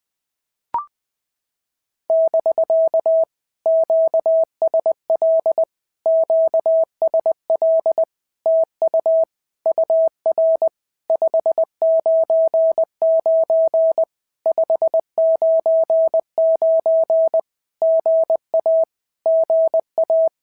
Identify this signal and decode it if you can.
Morse code, heard as a keyed tone, about 20 wpm, 650 Hz. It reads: <BK> QSL QSL TU UR 599 599 GA GA
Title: POTA QSO Practice - 20wpm